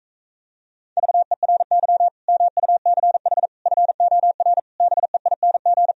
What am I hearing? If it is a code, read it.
Morse code, heard as a keyed tone, about 45 wpm, 700 Hz. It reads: VERY MUCH FOR BEING